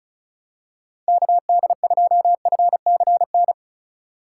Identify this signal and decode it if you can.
Morse code, heard as a keyed tone, about 36 wpm, 700 Hz. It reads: KD2FCN
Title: USA Call Signs - 6 Characters - 3x Character Spacing - 35wpm